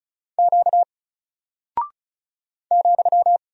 Morse code, heard as a keyed tone, about 36 wpm, 700 Hz. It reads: Q ,
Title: Single Character (Letter:Number:Punctuation:Prosign) Alt 35wpm